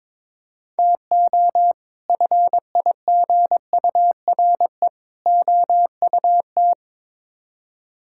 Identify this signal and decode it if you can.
Morse code, heard as a keyed tone, about 22 wpm, 700 Hz. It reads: TO FIGURE OUT
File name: practice / 22wpm / Build A Phrase - 5-Word Phrases - 22wpm